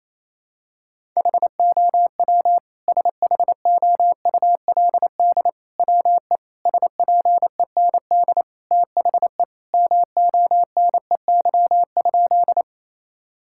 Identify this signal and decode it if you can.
Morse code, heard as a keyed tone, about 28 wpm, 700 Hz. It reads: HOW SHOULD WE SPEND THE MONEY?
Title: Sentences from Top 1000 Words - Review Part 5 - 28wpm